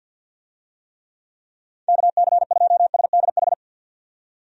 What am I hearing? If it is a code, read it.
Morse code, heard as a keyed tone, about 51 wpm, 700 Hz. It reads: KC2SDH
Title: USA Call Signs - 6 Characters - 50wpm